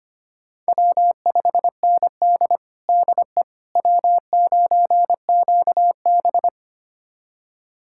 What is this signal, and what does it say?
Morse code, heard as a keyed tone, about 25 wpm, 700 Hz. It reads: W5ND DE W9QB
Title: USA Callsign DE Callsign - 4 Characters - 25wpm